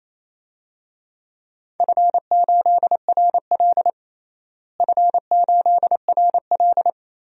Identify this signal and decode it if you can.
Morse code, heard as a keyed tone, about 28 wpm, 700 Hz. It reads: F8RL F8RL
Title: DXCC-100 Call Signs - 4 Characters - Repeated 2x - 28wpm